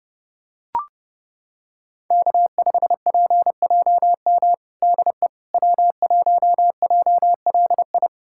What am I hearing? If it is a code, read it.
Morse code, heard as a keyed tone, about 30 wpm, 700 Hz. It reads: K5PJM DE W1JLI
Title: USA Callsign DE Callsign - 5 Characters - 30wpm